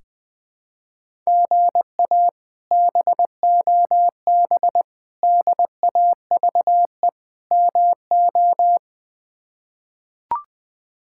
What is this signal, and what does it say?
Morse code, heard as a keyed tone, about 20 wpm, 700 Hz. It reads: GA BOB DAVE MO
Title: K1USN SST QSO Practice - 20wpm